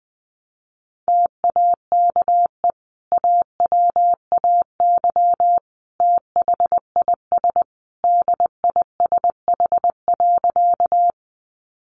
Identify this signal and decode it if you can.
Morse code, heard as a keyed tone, about 19 wpm, 700 Hz. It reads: TAKE AWAY THIS DISH.
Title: Sentences from Top 1300 Words 20wpm